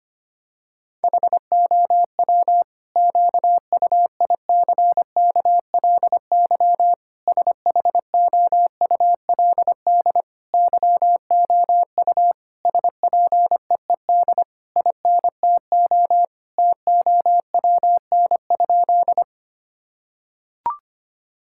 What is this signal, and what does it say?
Morse code, heard as a keyed tone, about 25 wpm, 700 Hz. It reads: HOW QUICKLY SHOULD YOU SPEED INTO TOWN?
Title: Sentences from Top 1000 Words - Review Part 5 - 25wpm